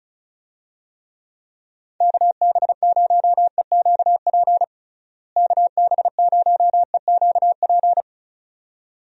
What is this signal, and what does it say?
Morse code, heard as a keyed tone, about 36 wpm, 700 Hz. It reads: KB0EQP KB0EQP
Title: USA Call Signs - 6 Characters - Repeated 2x - 35wpm